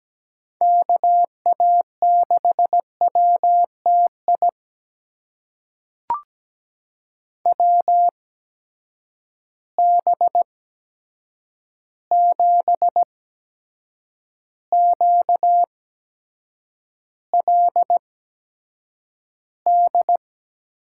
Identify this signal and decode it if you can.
Morse code, heard as a keyed tone, about 17 wpm, spaced as for 7 wpm, 700 Hz. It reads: KA6WTI WB7QLD
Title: USA Call Signs - 6 Characters - 8x Character Spacing - 17wpm